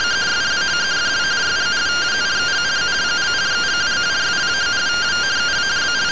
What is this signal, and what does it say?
Two-tone teleprinter signal, about 170 Hz shift.